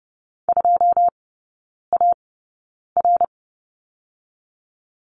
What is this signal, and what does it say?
Morse code, heard as a keyed tone, about 30 wpm, spaced as for 9 wpm, 700 Hz. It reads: 2AR
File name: Lesson 19e - Letter w - Three Characters 30wpm